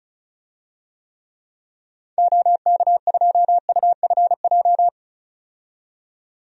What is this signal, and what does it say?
Morse code, heard as a keyed tone, about 36 wpm, 700 Hz. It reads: OK2UFJ
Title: DXCC-100 Call Signs - 6 Characters - 35wpm